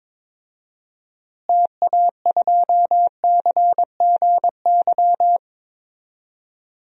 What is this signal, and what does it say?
Morse code, heard as a keyed tone, about 22 wpm, 700 Hz. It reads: TA2CGY
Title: DXCC-100 Call Signs - 6 Characters - Repeated 2x - 22wpm